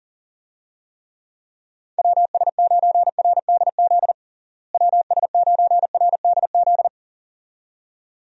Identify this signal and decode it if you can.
Morse code, heard as a keyed tone, about 40 wpm, 700 Hz. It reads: WS9RDZ WS9RDZ
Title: USA Call Signs - 6 Characters - Repeated 2x - 40wpm